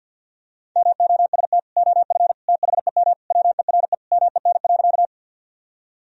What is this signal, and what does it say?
Morse code, heard as a keyed tone, about 51 wpm, 700 Hz. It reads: MOST OF THEM WERE MEN.
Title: Sentences from Top 1000 Words - Review Part 1 - 50wpm